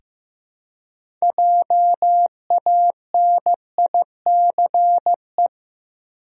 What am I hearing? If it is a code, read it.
Morse code, heard as a keyed tone, about 15 wpm, 700 Hz. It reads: JANICE
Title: USA Common Names - 15wpm